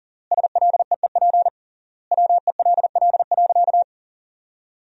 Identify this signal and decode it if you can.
Morse code, heard as a keyed tone, about 40 wpm, 700 Hz. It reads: SLEEP WELL.
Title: 3x Word-Spacing - Sentences from Top 500 Words - 40wpm